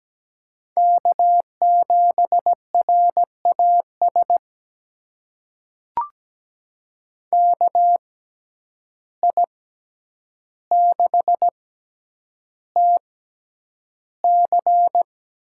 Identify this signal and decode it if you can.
Morse code, heard as a keyed tone, about 17 wpm, spaced as for 9 wpm, 700 Hz. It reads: K7RAS KI6TC
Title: USA Call Signs - 5 Characters - 6x Character Spacing - 17wpm